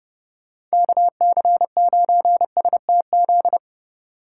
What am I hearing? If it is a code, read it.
Morse code, heard as a keyed tone, about 31 wpm, 700 Hz. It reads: KC9STZ
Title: Speed Racing - USA 2x3 Call Signs 25wpm